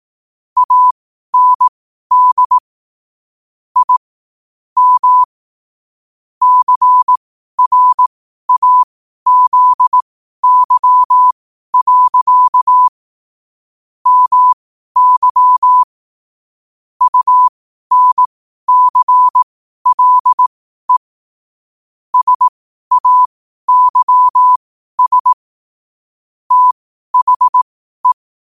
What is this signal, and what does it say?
Morse code, heard as a keyed tone, about 18 wpm, spaced as for 12 wpm, 1000 Hz. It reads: AND I M CRAZY. MY UNCLE SAYS THE
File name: qotdfiles / 12wpm